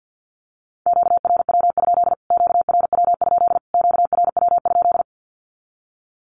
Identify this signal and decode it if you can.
Morse code, heard as a keyed tone, about 50 wpm, 700 Hz. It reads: QRW? QRW? QRW?
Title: Q-Codes Repeated 3x 50wpm